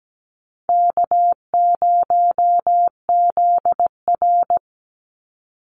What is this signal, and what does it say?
Morse code, heard as a keyed tone, about 17 wpm, 700 Hz. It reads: K0ZR
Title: Easy Call Signs Repeated 2x - 17wpm